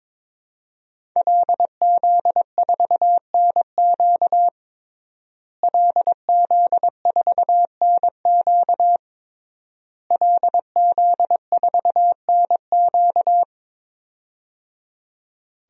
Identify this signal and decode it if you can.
Morse code, heard as a keyed tone, about 22 wpm, 700 Hz. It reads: LZ4NQ LZ4NQ LZ4NQ
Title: DXCC-100 Call Signs - 5 Characters - Repeated 3x - 22wpm